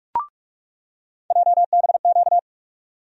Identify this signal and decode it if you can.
Morse code, heard as a keyed tone, about 42 wpm, 700 Hz. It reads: JBQ
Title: USA Call Sign Postfixes - 3 Letters 45wpm